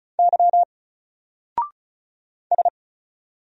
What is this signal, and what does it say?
Morse code, heard as a keyed tone, about 33 wpm, 700 Hz. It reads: Y S
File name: Single-Letter-Number-35wpm